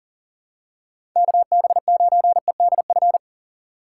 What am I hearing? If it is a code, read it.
Morse code, heard as a keyed tone, about 40 wpm, 700 Hz. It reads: KB9EDF
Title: USA Call Signs - 6 Characters - 2x Character Spacing - 40wpm